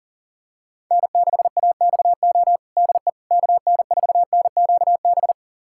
Morse code, heard as a keyed tone, about 37 wpm, 700 Hz. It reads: N6AXO DE KN4NQB
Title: USA Callsign DE Callsign 40wpm